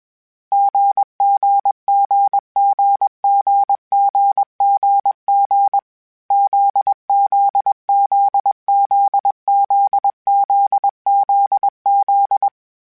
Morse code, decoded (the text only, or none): GGGGGGGG ZZZZZZZZ